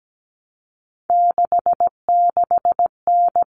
Morse code, heard as a keyed tone, about 17 wpm, 700 Hz. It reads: 66N
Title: All Cut Numbers - 3-Digit 17wpm